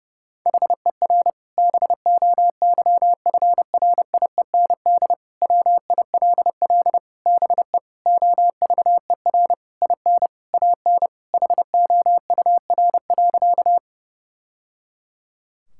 Morse code, heard as a keyed tone, about 30 wpm, 700 Hz. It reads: HER BOYFRIEND WILL BE OVER IN AN HOUR.
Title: Sentences from Top 1000 Words - Review Part 5 - 30wpm